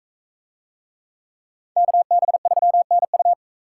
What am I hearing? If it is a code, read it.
Morse code, heard as a keyed tone, about 40 wpm, 700 Hz. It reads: KB3NU